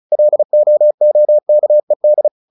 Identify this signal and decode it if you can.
Morse code, heard as a keyed tone, about 34 wpm, 600 Hz. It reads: LOOKED
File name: the old man and the sea / the old man and the sea 1 35wpm